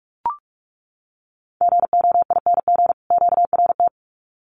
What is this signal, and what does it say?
Morse code, heard as a keyed tone, about 44 wpm, 700 Hz. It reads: GOING QRT
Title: Common QSO Phrases 45wpm